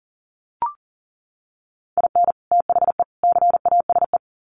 Morse code, heard as a keyed tone, about 40 wpm, 700 Hz. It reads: IN THE CASE
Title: Sets of 3 Words 40wpm